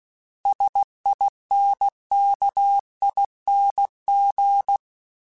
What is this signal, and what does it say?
Morse code, heard as a keyed tone, about 16 wpm, 800 Hz. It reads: SINKING
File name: Special INGs / Special INGs-mp3s / Sngl ing @15